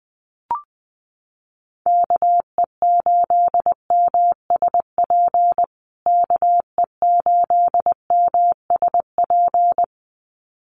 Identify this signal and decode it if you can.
Morse code, heard as a keyed tone, about 20 wpm, 700 Hz. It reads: KE8MSP KE8MSP
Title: USA Call Signs Repeated 2x 20wpm